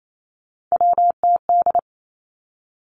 Morse code, heard as a keyed tone, about 28 wpm, 700 Hz. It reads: WTD
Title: Three Challenging Letter-Number Combinations 28wpm